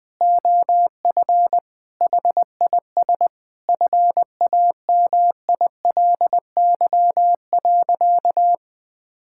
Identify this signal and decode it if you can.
Morse code, heard as a keyed tone, about 21 wpm, 700 Hz. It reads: OF HIS FAMILY.
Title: Sentences from Top 1000 Words - Review Part 5 - 20wpm